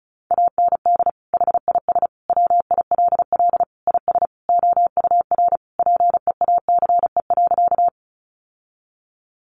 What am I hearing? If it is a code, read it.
Morse code, heard as a keyed tone, about 34 wpm, 700 Hz. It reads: AND HIS WILL IS OUR PEACE.
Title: Sentences from Top 1500 Words 35wpm